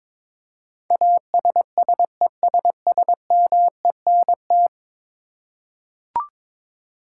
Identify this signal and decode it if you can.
Morse code, heard as a keyed tone, about 22 wpm, 700 Hz. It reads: ASSESSMENT
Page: morse-code-ninja-practice-22wpm